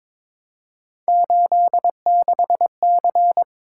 Morse code, heard as a keyed tone, about 26 wpm, 700 Hz. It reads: 86C
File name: Speed-Racing - Three Letter-Number Combinations - 22wpm